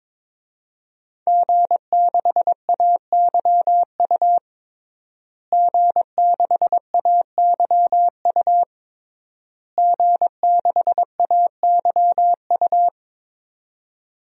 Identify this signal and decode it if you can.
Morse code, heard as a keyed tone, about 22 wpm, 700 Hz. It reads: G6AYU G6AYU G6AYU
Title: DXCC-100 Call Signs - 5 Characters - Repeated 3x - 22wpm